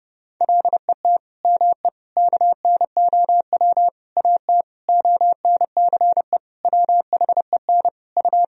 Morse code, heard as a keyed tone, about 30 wpm, 700 Hz. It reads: LET ME KNOW AT ONCE WHEN U
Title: Sentences from Top 300 Words and Abbreviations 30wpm